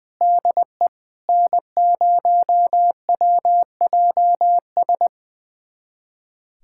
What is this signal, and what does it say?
Morse code, heard as a keyed tone, about 20 wpm, 700 Hz. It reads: DE N0WJS
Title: USA Callsign DE Callsign - 5 Characters - 20wpm